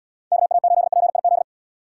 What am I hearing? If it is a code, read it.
Morse code, heard as a keyed tone, about 76 wpm, 700 Hz. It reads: KA8PIZ